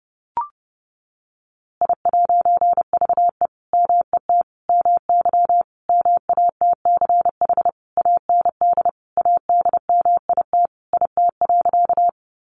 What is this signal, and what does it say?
Morse code, heard as a keyed tone, about 30 wpm, 700 Hz. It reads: I'VE MET MY MATCH AND ADMIT IT.
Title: A Sound Of Thunder 30wpm